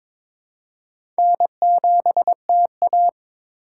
Morse code, heard as a keyed tone, about 22 wpm, 700 Hz. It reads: N7TA
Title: USA Call Signs - 4x Character Spacing - 22wpm